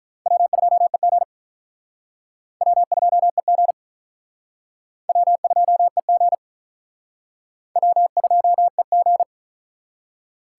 Morse code, heard as a keyed tone, about 44 wpm, 700 Hz. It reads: W2EG W2EG W2EG W2EG